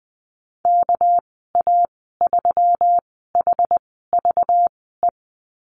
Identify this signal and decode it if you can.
Morse code, heard as a keyed tone, about 20 wpm, spaced as for 14 wpm, 700 Hz. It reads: KA3HVE
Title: 213 Lesson 34g - Number 8 - Call Signs 20wpm